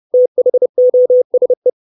Morse code, heard as a keyed tone, about 30 wpm, 500 Hz. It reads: THOSE
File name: 632 Words 30WPM